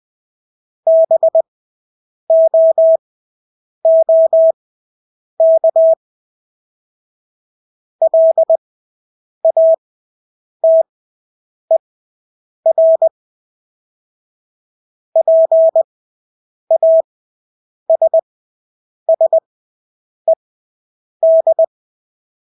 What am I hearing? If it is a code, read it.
Morse code, heard as a keyed tone, about 20 wpm, spaced as for 8 wpm, 650 Hz. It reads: BOOK LATER PASSED
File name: Engelske ord 20/8 wpm